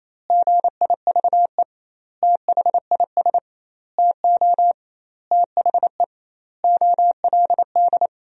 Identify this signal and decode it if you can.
Morse code, heard as a keyed tone, about 27 wpm, 700 Hz. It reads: GIVE THIS TO THE OLD
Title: 2x Word-Spacing - Sentences from Top 200 Words - 28wpm